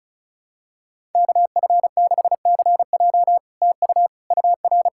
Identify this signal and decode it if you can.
Morse code, heard as a keyed tone, about 36 wpm, 700 Hz. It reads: KF6CJ TU UR
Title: SOTA QSO Practice 35wpm